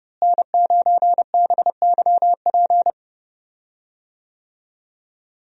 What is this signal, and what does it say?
Morse code, heard as a keyed tone, about 31 wpm, 700 Hz. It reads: N9BYP
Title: USA Call Signs - 5 Characters - Repeated 2x - 30wpm